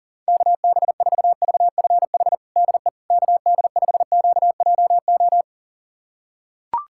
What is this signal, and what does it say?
Morse code, heard as a keyed tone, about 40 wpm, 700 Hz. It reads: KB4VFH DE KD5QJO
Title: USA Callsign DE Callsign 40wpm